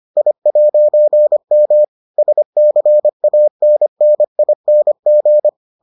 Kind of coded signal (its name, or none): Morse code